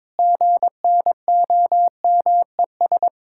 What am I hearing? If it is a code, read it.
Morse code, heard as a keyed tone, about 22 wpm, 700 Hz. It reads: GNOMES
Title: Pangram Challenge v2 - 22wpm